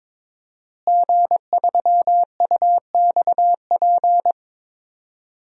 Morse code, heard as a keyed tone, about 22 wpm, 700 Hz. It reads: G3UXP
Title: DXCC-100 Call Signs - 5 Characters - Repeated 2x - 22wpm